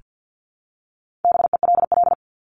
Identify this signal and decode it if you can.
Morse code, heard as a keyed tone, about 48 wpm, 700 Hz. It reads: BELL
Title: Top 1751-2000 Words 50wpm